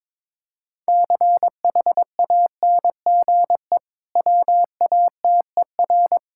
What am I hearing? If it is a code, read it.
Morse code, heard as a keyed tone, about 22 wpm, 690 Hz. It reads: CHANGE WATER
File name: practice / 22wpm / Easy Sentences with Hints - 22wpm